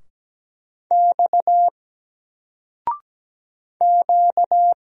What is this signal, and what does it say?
Morse code, heard as a keyed tone, about 17 wpm, 700 Hz. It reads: X Q
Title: Single Letter-Number Alt 17wpm